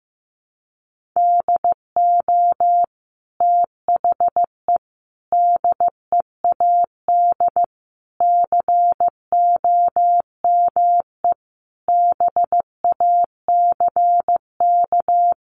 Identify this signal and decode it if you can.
Morse code, heard as a keyed tone, about 15 wpm, 700 Hz. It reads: DO THE DEAD COME BACK
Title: Sentences from Top 1500 Words 15wpm